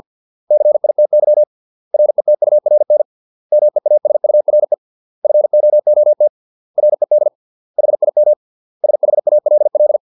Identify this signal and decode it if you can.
Morse code, heard as a keyed tone, about 50 wpm, 600 Hz. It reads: CITY RETURN MEASURE FOOT RED HIM SHALL
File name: 632 words / 632 Words 50WPM